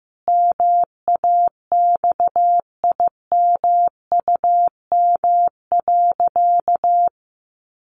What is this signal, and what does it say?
Morse code, heard as a keyed tone, about 15 wpm, 700 Hz. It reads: MAXIMUM.
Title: Sentences from Top 1300 Words 15wpm